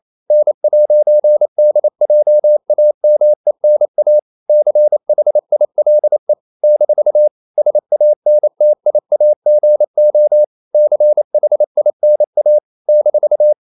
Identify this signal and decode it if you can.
Morse code, heard as a keyed tone, about 28 wpm, 600 Hz. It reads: N'DJAMENA CHILE - SANTIAGO CHINA -
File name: CW / Practice / NationalCapitals28